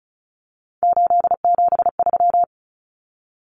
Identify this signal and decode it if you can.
Morse code, heard as a keyed tone, about 34 wpm, 700 Hz. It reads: 873
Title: Numbers - 3-Digit 35wpm